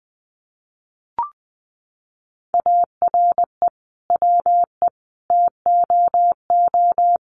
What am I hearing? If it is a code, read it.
Morse code, heard as a keyed tone, about 20 wpm, 700 Hz. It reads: ARE WE TOO
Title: Sentences from Top 300 Words 20wpm